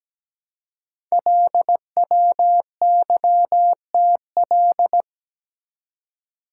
Morse code, heard as a keyed tone, about 17 wpm, 700 Hz. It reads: LWYTL
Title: Code Groups - 5 Letters - 17wpm